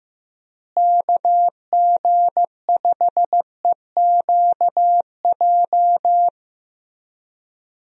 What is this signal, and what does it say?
Morse code, heard as a keyed tone, about 15 wpm, 700 Hz. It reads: KG5EQJ
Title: USA Call Signs - 6 Characters - Repeated 3x - 15wpm